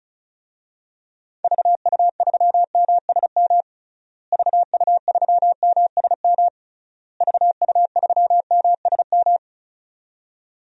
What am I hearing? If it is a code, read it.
Morse code, heard as a keyed tone, about 36 wpm, 700 Hz. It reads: VU3MSM VU3MSM VU3MSM
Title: DXCC-100 Call Signs - 6 Characters - Repeated 3x - 35wpm